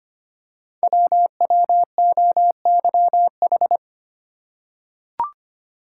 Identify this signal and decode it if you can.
Morse code, heard as a keyed tone, about 25 wpm, 700 Hz. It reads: WWOYH
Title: Code Groups - 5 Letters - 25wpm